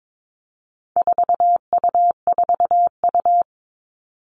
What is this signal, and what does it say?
Morse code, heard as a keyed tone, about 22 wpm, 700 Hz. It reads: 4U4U
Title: All Cut Numbers - 4-Digit 22wpm